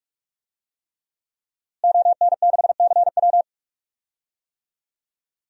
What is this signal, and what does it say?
Morse code, heard as a keyed tone, about 45 wpm, 700 Hz. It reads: ON6CW
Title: DXCC-100 Call Signs - 5 Characters - Repeated 3x - 45wpm